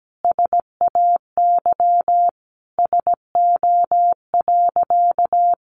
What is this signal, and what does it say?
Morse code, heard as a keyed tone, about 17 wpm, 700 Hz. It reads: SAY SO.
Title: Sentences from Top 1400 Words 17wpm